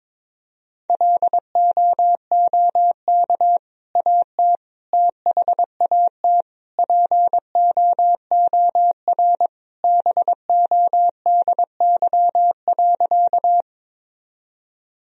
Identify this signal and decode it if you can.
Morse code, heard as a keyed tone, about 22 wpm, 700 Hz. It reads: LOOK AT THAT POOR BODY.
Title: Sentences from Top 1000 Words - Review Part 2 - 22wpm